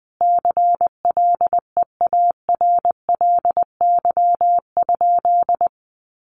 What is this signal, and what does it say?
Morse code, heard as a keyed tone, about 20 wpm, 700 Hz. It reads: CLEARLY?
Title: Sentences from Top 1300 Words 20wpm